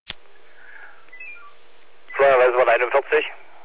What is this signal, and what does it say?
Five-tone selective call (ZVEI), heard as quick runs of five tones, tones of about 70 ms, with speech. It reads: 88042